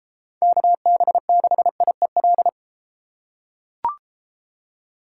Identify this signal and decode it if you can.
Morse code, heard as a keyed tone, about 33 wpm, 700 Hz. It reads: KB6IEL